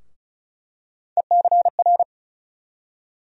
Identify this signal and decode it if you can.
Morse code, heard as a keyed tone, about 36 wpm, 700 Hz. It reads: ECR
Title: USA Call Sign Postfixes - 3 Letters 35wpm